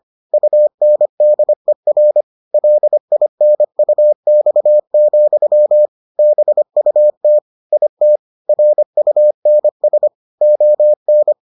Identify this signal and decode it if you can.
Morse code, heard as a keyed tone, about 25 wpm, 600 Hz. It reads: UNDER LINUX, BUT IT RUNS ON